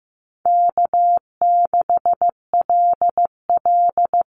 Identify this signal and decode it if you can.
Morse code, heard as a keyed tone, about 15 wpm, 700 Hz. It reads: K6LL
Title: Easy Call Signs Repeated 2x - 15wpm